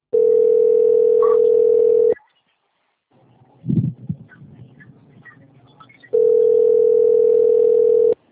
Ringback tone (North America), 440 + 480 Hz.